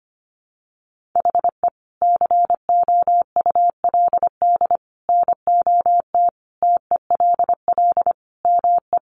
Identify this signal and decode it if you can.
Morse code, heard as a keyed tone, about 25 wpm, 700 Hz. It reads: HE COULD NOT TELL ME
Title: Sentences from Top 600 Words 25wpm